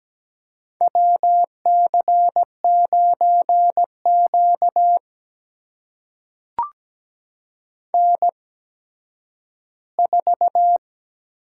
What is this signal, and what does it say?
Morse code, heard as a keyed tone, about 17 wpm, spaced as for 10 wpm, 700 Hz. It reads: WC9Q N4
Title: USA Call Signs - 8x Character Spacing - 17wpm